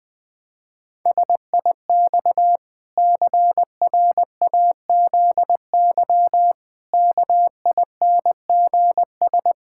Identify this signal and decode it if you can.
Morse code, heard as a keyed tone, about 20 wpm, 700 Hz. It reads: SIX CRAZY KINGS
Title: Pangram Challenge v2 - 20wpm